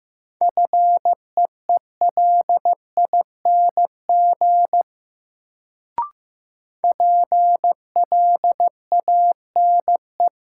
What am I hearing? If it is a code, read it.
Morse code, heard as a keyed tone, about 15 wpm, 700 Hz. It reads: FEELING PLANE
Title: Sentences from Top 1000 Words - Review Part 3 - New Word Drill - 15wpm